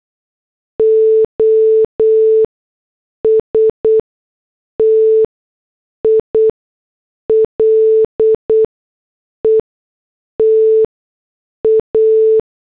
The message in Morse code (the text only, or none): OSTILE TA